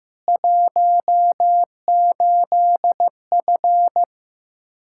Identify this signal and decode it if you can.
Morse code, heard as a keyed tone, about 15 wpm, 700 Hz. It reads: 18F
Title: K1USN SST QSO Practice 15wpm